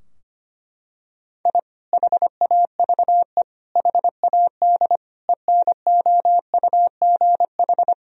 Morse code, heard as a keyed tone, about 25 wpm, 700 Hz. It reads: I HAVE HAD ENOUGH
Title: Sentences from Top 1000 Words - Review Part 3 - 25wpm